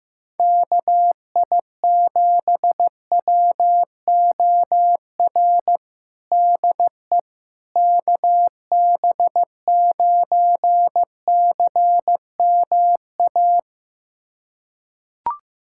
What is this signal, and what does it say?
Morse code, heard as a keyed tone, about 15 wpm, 700 Hz. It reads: KI7WOR DE KB9CMA
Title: USA Callsign DE Callsign - 6 Characters - 15wpm